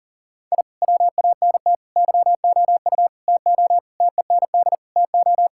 Morse code, heard as a keyed tone, about 40 wpm, 700 Hz. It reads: I WANT YOU TO TEND TO
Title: Sentences from Top 1000 Words - Review Part 5 - 40wpm